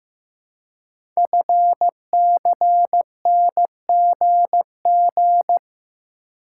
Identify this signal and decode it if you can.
Morse code, heard as a keyed tone, about 15 wpm, 700 Hz. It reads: FCNGG
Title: Code Groups - 5 Letters - 15wpm